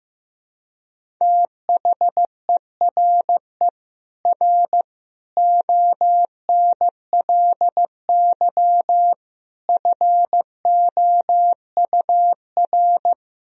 Morse code, heard as a keyed tone, about 15 wpm, 700 Hz. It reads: THERE R ONLY FOUR
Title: Sentences from Top 300 Words and Abbreviations 15wpm